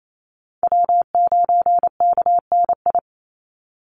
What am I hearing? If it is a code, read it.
Morse code, heard as a keyed tone, about 28 wpm, 700 Hz. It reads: W9KNI
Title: Call Signs - 28wpm